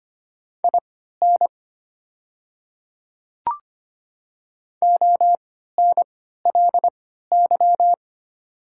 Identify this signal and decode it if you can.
Morse code, heard as a keyed tone, about 25 wpm, spaced as for 14 wpm, 700 Hz. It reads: IN ONLY
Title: 3x Character Spacing - Top 200 Words - 25wpm